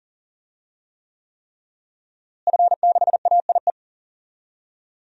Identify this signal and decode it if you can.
Morse code, heard as a keyed tone, about 40 wpm, 700 Hz. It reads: F6AIE
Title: DXCC-100 Call Signs - 5 Characters - Repeated 3x - 40wpm